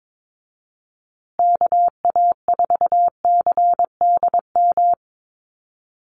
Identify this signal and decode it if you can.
Morse code, heard as a keyed tone, about 22 wpm, 700 Hz. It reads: KA4CDM
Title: USA Call Signs Repeated 2x 22wpm